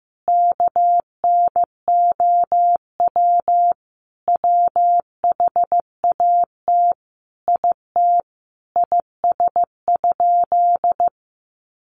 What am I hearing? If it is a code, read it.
Morse code, heard as a keyed tone, about 15 wpm, 700 Hz. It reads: KNOW WHAT IT IS?
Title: Sentences from Top 100 Words 15wpm